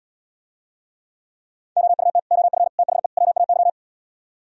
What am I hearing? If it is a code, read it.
Morse code, heard as a keyed tone, about 77 wpm, 700 Hz. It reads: OUT OF THE WAY